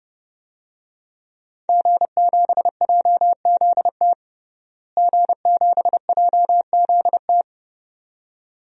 Morse code, heard as a keyed tone, about 29 wpm, 700 Hz. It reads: G7JZT G7JZT
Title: DXCC-100 Call Signs - 5 Characters - Repeated 2x - 30wpm